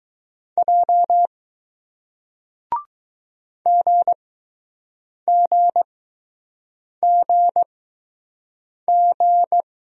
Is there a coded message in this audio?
Yes: Morse code from 0.6 to 9.6 s, heard as a keyed tone, about 20 wpm, spaced as for 6 wpm, 700 Hz. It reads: J GGGG